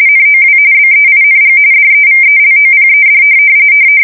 RTTY, about 45.45 baud. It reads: NOW BEING DEVELOPED
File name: fsk170 / printing-telegraph